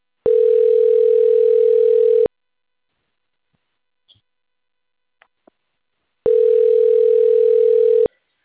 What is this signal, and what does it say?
Ringback tone (North America), 440 + 480 Hz.